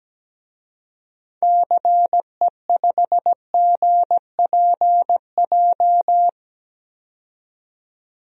Morse code, heard as a keyed tone, about 17 wpm, 700 Hz. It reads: CE5GPJ
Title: DXCC-100 Call Signs - 6 Characters - 17wpm